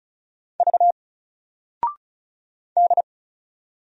Morse code, heard as a keyed tone, about 34 wpm, 700 Hz. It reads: V D
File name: Single-Letter-Number-35wpm